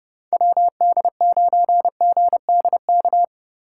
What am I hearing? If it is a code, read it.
Morse code, heard as a keyed tone, about 35 wpm, 700 Hz. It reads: WD9GDK
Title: Speed Racing - USA 2x3 Call Signs 30wpm